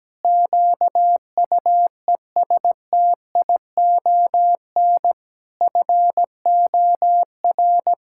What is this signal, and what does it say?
Morse code, heard as a keyed tone, about 17 wpm, 700 Hz. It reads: QUESTION FOR
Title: Sentences from Top 1000 Words - Review Part 6 - 17wpm